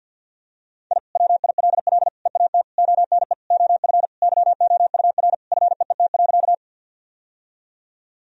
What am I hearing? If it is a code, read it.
Morse code, heard as a keyed tone, about 50 wpm, 700 Hz. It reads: I WILL EAT ONE OF YOUR FEET.
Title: Sentences from Top 1000 Words - Review Part 1 - 50wpm